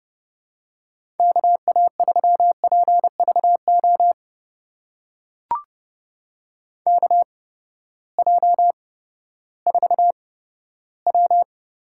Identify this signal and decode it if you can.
Morse code, heard as a keyed tone, about 31 wpm, spaced as for 15 wpm, 700 Hz. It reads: KA3PVO KJ4W
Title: USA Call Signs - 6 Characters - 8x Character Spacing - 30wpm